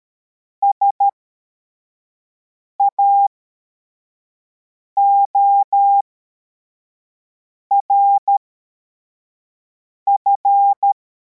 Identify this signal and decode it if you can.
Morse code, heard as a keyed tone, about 13 wpm, spaced as for 4 wpm, 800 Hz. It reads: SAORF
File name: formation / cw / Text61